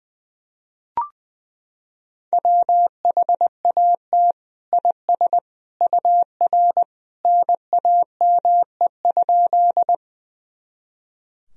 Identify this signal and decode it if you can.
Morse code, heard as a keyed tone, about 21 wpm, 700 Hz. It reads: WHAT IS UR NAME?
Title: Sentences from Top 200 Words and Abbreviations 20wpm